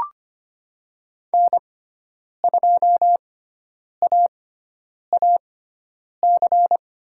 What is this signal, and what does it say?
Morse code, heard as a keyed tone, about 25 wpm, spaced as for 9 wpm, 700 Hz. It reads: N2AAC
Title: USA Call Signs - 5 Characters - 6x Character Spacing - 25wpm